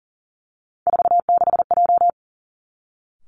Morse code, heard as a keyed tone, about 39 wpm, 700 Hz. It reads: 46J
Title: Three Challenging Letter-Number Combinations 40wpm